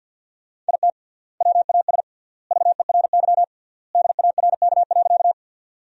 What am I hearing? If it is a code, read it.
Morse code, heard as a keyed tone, about 47 wpm, 700 Hz. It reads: IT WAS VERY DARK.
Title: 3x Word-Spacing - Sentences from Top 500 Words - 50wpm